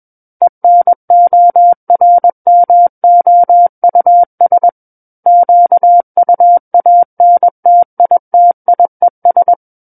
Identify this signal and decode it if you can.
Morse code, heard as a keyed tone, about 21 wpm, 700 Hz. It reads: ENORMOUS QUANTITIES
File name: Book One / 12-Chapter 12 - 21 WPM